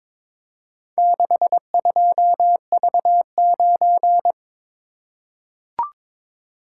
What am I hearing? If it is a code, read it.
Morse code, heard as a keyed tone, about 23 wpm, 700 Hz. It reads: 62V9
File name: practice / 22wpm / Four Letter-Number Combinations - 22wpm